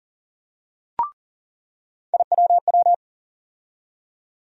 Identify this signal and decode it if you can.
Morse code, heard as a keyed tone, about 40 wpm, 700 Hz. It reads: IWW
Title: USA Call Sign Postfixes - 3 Letters 40wpm